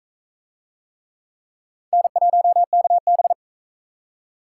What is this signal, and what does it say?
Morse code, heard as a keyed tone, about 44 wpm, 700 Hz. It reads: N1KB